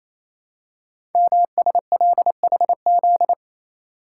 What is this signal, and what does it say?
Morse code, heard as a keyed tone, about 28 wpm, 700 Hz. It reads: MSLHZ
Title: Code Groups - 5 Letter:Numbers - 28wpm